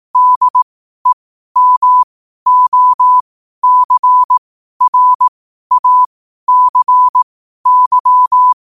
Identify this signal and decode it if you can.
Morse code, heard as a keyed tone, about 18 wpm, spaced as for 12 wpm, 1000 Hz. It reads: DEMOCRACY
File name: qotdfiles / 12wpm